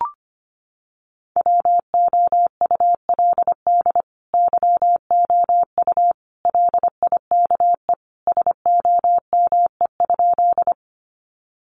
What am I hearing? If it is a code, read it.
Morse code, heard as a keyed tone, about 25 wpm, 700 Hz. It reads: WOULD YOU LIKE SOME?
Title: Sentences from Top 100 Words 25wpm